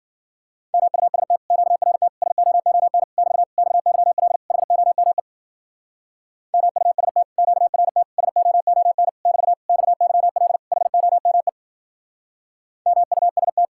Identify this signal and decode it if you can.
Morse code, heard as a keyed tone, about 50 wpm, 700 Hz. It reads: MUST QRT SOON <BT> XYL HOME MUST QRT SOON <BT> XYL HOME MUST
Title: Speed-Racing Common QSO Phrases - 40wpm